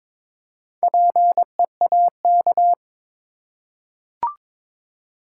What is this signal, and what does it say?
Morse code, heard as a keyed tone, about 21 wpm, 700 Hz. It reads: PEAK
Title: Four Letter Words - 22wpm